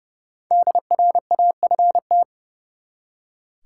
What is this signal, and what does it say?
Morse code, heard as a keyed tone, about 30 wpm, 700 Hz. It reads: DRAFT
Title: Five Letter Words - 30wpm